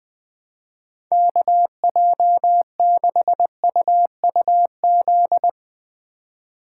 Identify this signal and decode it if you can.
Morse code, heard as a keyed tone, about 20 wpm, 700 Hz. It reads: KJ6UUZ
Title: Speed Racing - USA Call Signs 17wpm